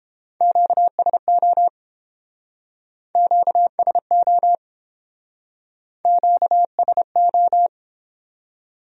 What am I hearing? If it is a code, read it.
Morse code, heard as a keyed tone, about 29 wpm, 700 Hz. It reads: QSO QSO QSO